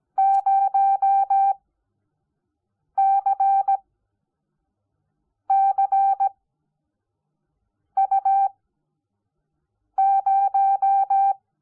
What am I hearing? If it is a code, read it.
Morse code, heard as a keyed tone, about 17 wpm, spaced as for 5 wpm, 780 Hz. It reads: 0CCU0
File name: Onlinelehrgang / morsen / DJ4UF Morsen05